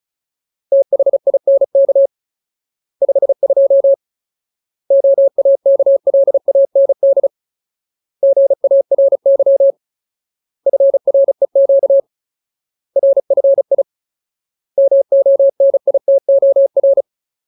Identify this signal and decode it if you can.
Morse code, heard as a keyed tone, about 34 wpm, 550 Hz. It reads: THINK 52 OAKLAND GARY FREQ RFI MONITOR
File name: Ultimate Word List / 35WPM Ultimate